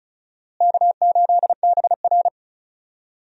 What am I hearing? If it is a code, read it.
Morse code, heard as a keyed tone, about 34 wpm, 700 Hz. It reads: K8BR
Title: USA Call Signs - 4 Characters - Repeated 3x - 35wpm